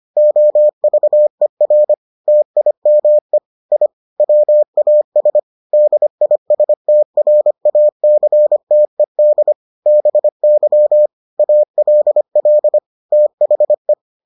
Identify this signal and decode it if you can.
Morse code, heard as a keyed tone, about 25 wpm, 600 Hz. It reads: OVER TIME I WAS DISTRACTED BY ALL THE